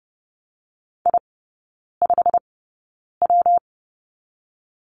Morse code, heard as a keyed tone, about 30 wpm, 700 Hz. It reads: I 5 W
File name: Lesson 19e - Letter w - Three Characters 30wpm